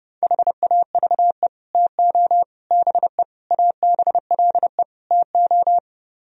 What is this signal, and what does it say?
Morse code, heard as a keyed tone, about 30 wpm, 700 Hz. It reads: HAVE TO BE ABLE TO